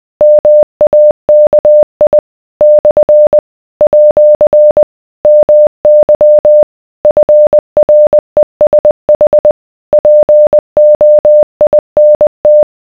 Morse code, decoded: MAKI / @ MY FRESH POINT